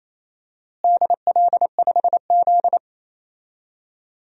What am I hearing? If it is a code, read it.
Morse code, heard as a keyed tone, about 28 wpm, 700 Hz. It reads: DL5Z
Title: DXCC-100 Call Signs - 4 Characters - Repeated 2x - 28wpm